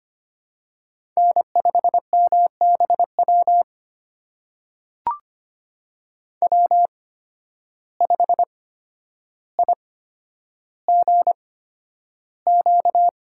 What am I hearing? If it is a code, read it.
Morse code, heard as a keyed tone, about 25 wpm, spaced as for 11 wpm, 700 Hz. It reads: N5MBW W5IGQ
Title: USA Call Signs - 5 Characters - 8x Character Spacing - 25wpm